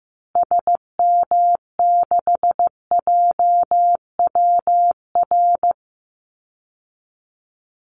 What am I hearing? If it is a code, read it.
Morse code, heard as a keyed tone, about 15 wpm, 700 Hz. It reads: SM6JWR
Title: Call Signs Repeated 3x - 15wpm